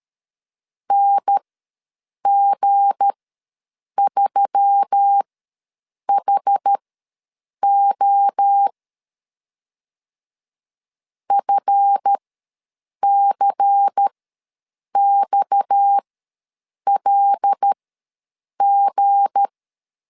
Morse code, heard as a keyed tone, about 13 wpm, spaced as for 7 wpm, 800 Hz. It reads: NG3HO FCXLG
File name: LEZIONI / 7 GF9Z / 7 GF ws 12wpm 3140